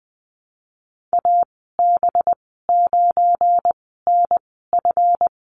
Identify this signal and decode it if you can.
Morse code, heard as a keyed tone, about 19 wpm, spaced as for 14 wpm, 700 Hz. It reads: AB9NF
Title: 186 Lesson 30g - Letter b - Call Signs 20wpm